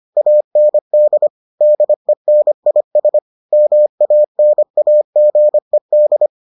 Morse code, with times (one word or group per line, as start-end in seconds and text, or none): 0.2-1.3 AND
1.6-3.2 DENIS
3.5-6.3 MANAGED